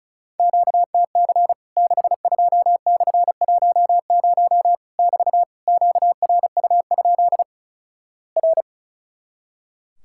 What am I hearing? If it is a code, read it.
Morse code, heard as a keyed tone, about 36 wpm, 700 Hz. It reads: QTC 62/10 <BT> QRU? R
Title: Worked All Europe DX Contest - QTC Exchange - 35wpm